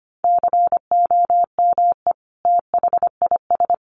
Morse code, heard as a keyed tone, about 25 wpm, 700 Hz. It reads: COME THIS
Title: Sentences from Top 100 Words 25wpm